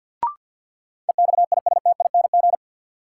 Morse code, heard as a keyed tone, about 47 wpm, 700 Hz. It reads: EXISTING
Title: Sentences from Top 1500 Words - New Word Drill - 50wpm